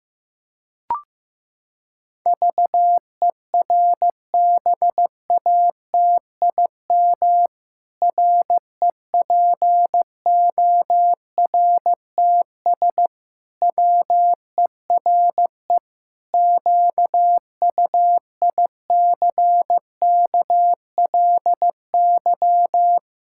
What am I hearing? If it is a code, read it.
Morse code, heard as a keyed tone, about 15 wpm, 700 Hz. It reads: VERBATIM REPORTS WERE QUICKLY
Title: Pangram Challenge v2 - 15wpm